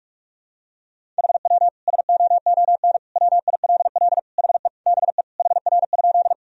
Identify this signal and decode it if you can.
Morse code, heard as a keyed tone, about 41 wpm, 700 Hz. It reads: HW SOON WILL HE BE HR?
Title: Sentences from Top 400 Words and Abbreviations 45wpm